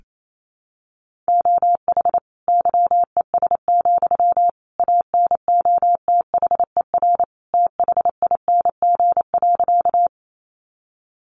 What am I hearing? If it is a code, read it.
Morse code, heard as a keyed tone, about 28 wpm, 700 Hz. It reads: OH YES, ANOTHER THING.
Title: Sentences from Top 600 Words 28wpm